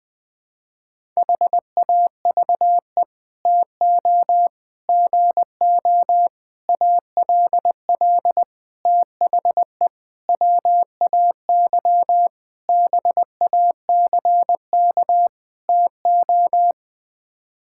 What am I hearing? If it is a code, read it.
Morse code, heard as a keyed tone, about 21 wpm, 700 Hz. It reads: HAVE TO GO ALL THE WAY BACK TO
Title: Sets of 8 Words - 20wpm